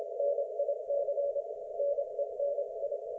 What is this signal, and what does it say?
Morse code, heard as a keyed tone, about 26 wpm, 570 Hz. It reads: NIC NENI
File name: Archiv / CUC216 / ok1fhi cul216